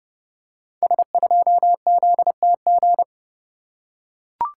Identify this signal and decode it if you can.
Morse code, heard as a keyed tone, about 30 wpm, 700 Hz. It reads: S2ZTG
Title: Code Groups - 5 Letter:Numbers - 30wpm